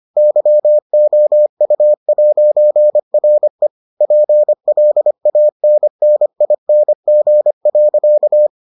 Morse code, heard as a keyed tone, about 25 wpm, 600 Hz. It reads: YOU'RE PLANNING.